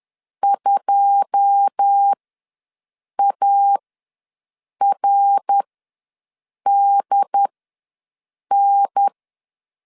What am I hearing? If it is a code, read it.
Morse code, heard as a keyed tone, about 10 wpm, spaced as for 6 wpm, 800 Hz. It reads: 2ARDN
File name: LEZIONI / 9 2 ws 10wpm 4130